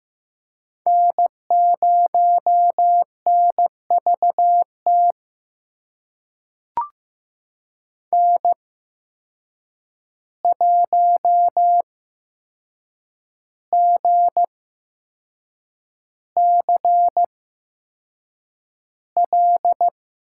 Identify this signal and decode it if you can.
Morse code, heard as a keyed tone, about 15 wpm, spaced as for 6 wpm, 700 Hz. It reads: N0NVT N1GCL
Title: USA Call Signs - 8x Character Spacing - 15wpm